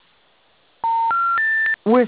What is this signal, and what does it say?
Special information tone, with speech.